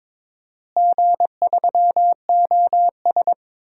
Morse code, heard as a keyed tone, about 21 wpm, 700 Hz. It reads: G3OS
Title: DXCC-100 Call Signs - 4 Characters - Repeated 2x - 22wpm